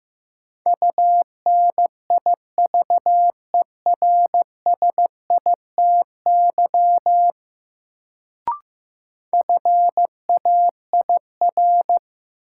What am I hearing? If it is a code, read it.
Morse code, heard as a keyed tone, about 15 wpm, 700 Hz. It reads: UNIVERSITY FAIR
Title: Sentences from Top 1300 Words - New Word Drill Alt - 15wpm